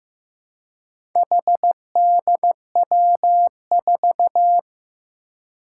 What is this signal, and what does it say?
Morse code, heard as a keyed tone, about 15 wpm, 700 Hz. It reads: HDW4
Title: Four Challenging Letter-Number Combinations 15wpm